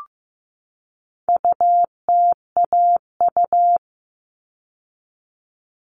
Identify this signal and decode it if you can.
Morse code, heard as a keyed tone, about 15 wpm, 700 Hz. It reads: UTAU
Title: All Cut Numbers - 4-Digit 15wpm